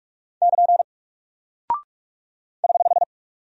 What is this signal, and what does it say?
Morse code, heard as a keyed tone, about 44 wpm, 700 Hz. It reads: <KN> <HH>
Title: Single Character (Letter:Number:Punctuation:Prosign) Alt 45wpm